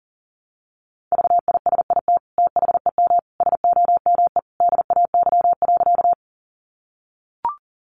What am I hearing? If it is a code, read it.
Morse code, heard as a keyed tone, about 40 wpm, 700 Hz. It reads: VISIT THEM SOME DAY.
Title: Sentences from Top 900 Words 40wpm